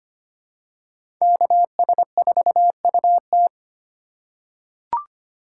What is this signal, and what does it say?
Morse code, heard as a keyed tone, about 25 wpm, 700 Hz. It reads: KS4UT
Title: USA Call Signs - 5 Characters - 4x Character Spacing - 25wpm